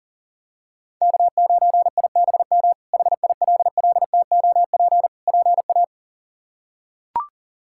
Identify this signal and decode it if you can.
Morse code, heard as a keyed tone, about 40 wpm, 700 Hz. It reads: K9IBM HILLTOP PA
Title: K1USN SST QSO Practice - 40wpm